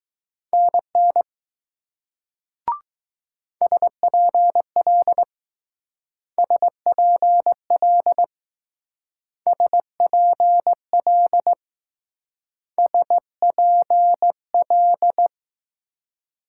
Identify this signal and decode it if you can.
Morse code, heard as a keyed tone, about 19 wpm, 700 Hz. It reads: NN SPL SPL SPL SPL